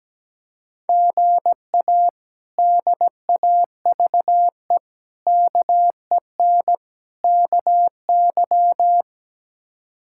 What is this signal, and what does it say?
Morse code, heard as a keyed tone, about 17 wpm, 700 Hz. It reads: GA DAVE KEN KY
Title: K1USN SST QSO Practice - 17wpm